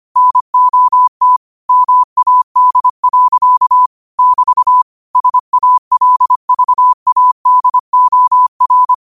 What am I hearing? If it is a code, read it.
Morse code, heard as a keyed tone, about 25 wpm, 1000 Hz. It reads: NOT MAD. <BT> SALVADOR